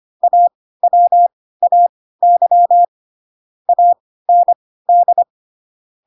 Morse code, heard as a keyed tone, about 25 wpm, spaced as for 16 wpm, 700 Hz. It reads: AWAY AND
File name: Pinocchio / Pinocchio ch3 10 25 ik1hge